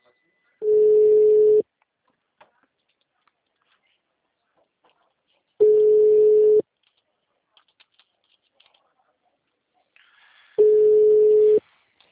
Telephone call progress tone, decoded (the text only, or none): ringback tone